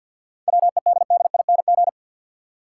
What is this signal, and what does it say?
Morse code, heard as a keyed tone, about 52 wpm, 700 Hz. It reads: WEDDING
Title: Seven Letter Words - 50wpm